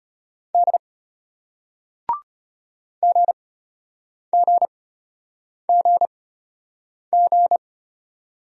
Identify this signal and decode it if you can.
Morse code, heard as a keyed tone, about 33 wpm, spaced as for 8 wpm, 700 Hz. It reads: D GGGG